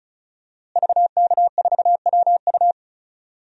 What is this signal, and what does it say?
Morse code, heard as a keyed tone, about 34 wpm, 700 Hz. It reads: VK4WU
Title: DXCC-100 Call Signs - 5 Characters - Repeated 3x - 35wpm